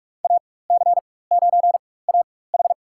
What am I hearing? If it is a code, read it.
Morse code, heard as a keyed tone, about 45 wpm, spaced as for 21 wpm, 690 Hz. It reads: AC9AH
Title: USA Call Signs - 4x Character Spacing - 45wpm